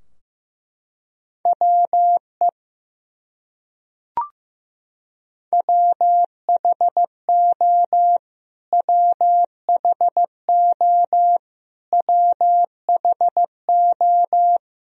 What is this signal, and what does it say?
Morse code, heard as a keyed tone, about 15 wpm, 700 Hz. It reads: WE WHO WHO WHO
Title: Top 100 Words Repeated 15wpm